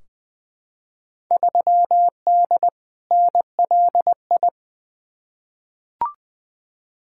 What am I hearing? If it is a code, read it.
Morse code, heard as a keyed tone, about 20 wpm, 700 Hz. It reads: 3D NLI
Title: Field Day Exchanges Encore - 20wpm